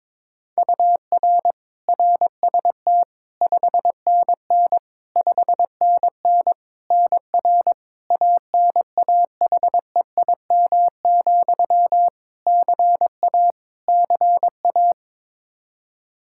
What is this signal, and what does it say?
Morse code, heard as a keyed tone, about 22 wpm, 700 Hz. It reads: UR RST 5NN 5NN NR ANAHEIM, CA CA
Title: Practice QSO - Exchange 1 - Beginning - 22wpm